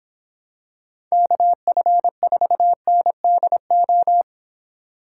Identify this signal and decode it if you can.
Morse code, heard as a keyed tone, about 25 wpm, 700 Hz. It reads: KF4NDO